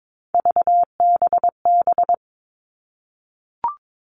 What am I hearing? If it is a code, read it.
Morse code, heard as a keyed tone, about 22 wpm, 700 Hz. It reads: VBB
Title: Three Challenging Letter-Number Combinations 22wpm